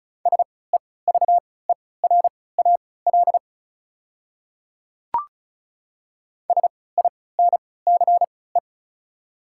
Morse code, heard as a keyed tone, about 34 wpm, spaced as for 20 wpm, 700 Hz. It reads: SEVERAL SINCE
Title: 3x Character Spacing - Top 500 Words - 35wpm